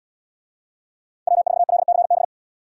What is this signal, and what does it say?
Morse code, heard as a keyed tone, about 74 wpm, 700 Hz. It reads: W4LCB